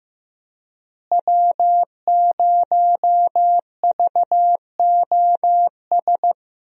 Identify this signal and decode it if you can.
Morse code, heard as a keyed tone, about 15 wpm, 700 Hz. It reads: W0VOS
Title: USA Call Signs - 5 Characters - Repeated 3x - 15wpm